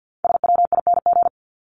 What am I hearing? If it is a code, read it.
Morse code, heard as a keyed tone, about 50 wpm, 700 Hz. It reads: SWING
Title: Top 1751-2000 Words 50wpm